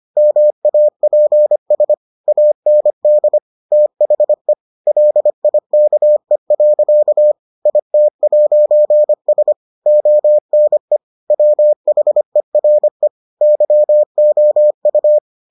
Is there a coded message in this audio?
Yes: Morse code from 0.2 to 15.2 s, heard as a keyed tone, about 25 wpm, 600 Hz. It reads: MAPS AND THE LIKE. IT'S ONE WHERE YOU